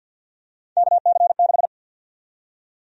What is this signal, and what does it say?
Morse code, heard as a keyed tone, about 53 wpm, 700 Hz. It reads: KC6
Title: USA Call Sign Prefixes - 50wpm